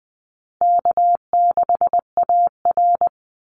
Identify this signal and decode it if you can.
Morse code, heard as a keyed tone, about 20 wpm, 700 Hz. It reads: K6AR
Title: Call Signs Repeated 2x - 20wpm